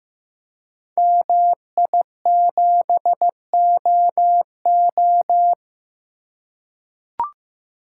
Morse code, heard as a keyed tone, about 15 wpm, 700 Hz. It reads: MI7OO
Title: Code Groups - 5 Letter:Numbers - 15wpm